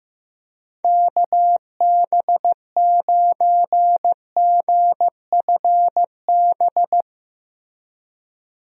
Morse code, heard as a keyed tone, about 16 wpm, 700 Hz. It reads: KB9GFB
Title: USA Call Signs - 6 Characters - Repeated 2x - 15wpm